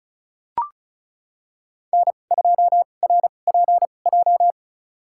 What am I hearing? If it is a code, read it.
Morse code, heard as a keyed tone, about 36 wpm, spaced as for 25 wpm, 700 Hz. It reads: N2RPJ
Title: USA Call Signs - 2x Character Spacing - 35wpm